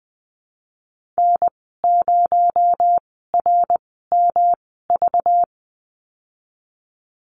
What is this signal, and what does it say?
Morse code, heard as a keyed tone, about 20 wpm, spaced as for 15 wpm, 700 Hz. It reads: N0RMV
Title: 231 Lesson 37g - Letter x - Call Signs 20wpm